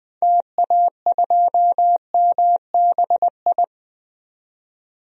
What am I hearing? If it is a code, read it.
Morse code, heard as a keyed tone, about 20 wpm, 700 Hz. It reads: TA2MBI
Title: DXCC-100 Call Signs - 6 Characters - 20wpm